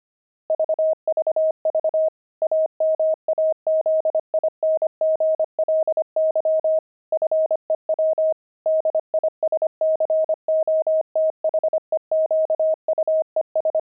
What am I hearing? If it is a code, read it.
Morse code, heard as a keyed tone, about 25 wpm, 630 Hz. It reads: VVV AMAZINGLY FEW DISCOTHEQUES